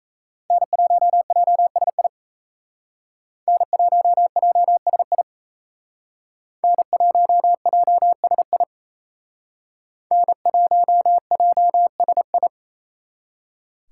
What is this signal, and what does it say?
Morse code, heard as a keyed tone, about 35 wpm, 700 Hz. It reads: N1JSI N1JSI N1JSI N1JSI